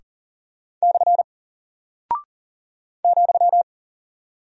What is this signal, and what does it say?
Morse code, heard as a keyed tone, about 38 wpm, 700 Hz. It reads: / ,
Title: Single Character (Letter:Number:Punctuation:Prosign) Alt 40wpm